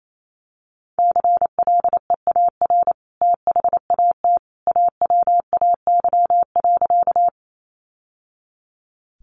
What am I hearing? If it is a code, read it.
Morse code, heard as a keyed tone, about 28 wpm, 700 Hz. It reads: CLEAR THAT AWAY.
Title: Sentences from Top 600 Words 28wpm